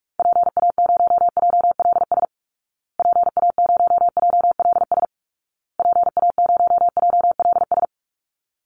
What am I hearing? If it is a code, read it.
Morse code, heard as a keyed tone, about 44 wpm, 700 Hz. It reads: PA0JLS PA0JLS PA0JLS
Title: Call Signs Repeated 2x - 45wpm